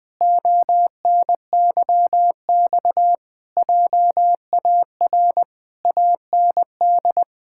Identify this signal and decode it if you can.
Morse code, heard as a keyed tone, about 21 wpm, 700 Hz. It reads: ONYX JAR AND
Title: Pangram Challenge v2 - 20wpm